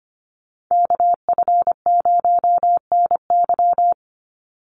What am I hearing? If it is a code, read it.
Morse code, heard as a keyed tone, about 25 wpm, 700 Hz. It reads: KF0NY
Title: USA Call Signs 25wpm